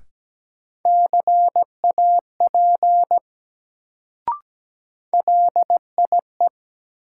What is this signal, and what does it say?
Morse code, heard as a keyed tone, about 17 wpm, 700 Hz. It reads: CAP LIE
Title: Sentences from Top 1600 Words - New Word Drill Alt - 17wpm